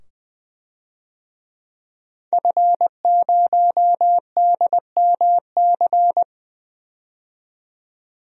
Morse code, heard as a keyed tone, about 20 wpm, 700 Hz. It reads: F0DMC
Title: DXCC-100 Call Signs - 5 Characters - 20wpm